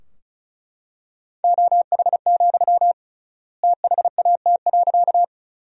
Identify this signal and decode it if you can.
Morse code, heard as a keyed tone, about 34 wpm, 700 Hz. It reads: OH, THAT.
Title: 3x Word-Spacing - Sentences from Top 500 Words - 35wpm